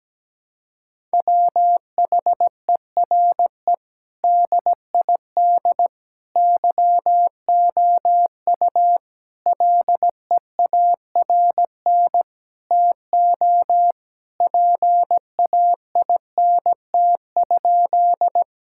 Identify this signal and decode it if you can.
Morse code, heard as a keyed tone, about 17 wpm, 700 Hz. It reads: WHERE DID YOU LEARN TO PAINT?
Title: Sentences from Top 1000 Words - Review Part 6 - 17wpm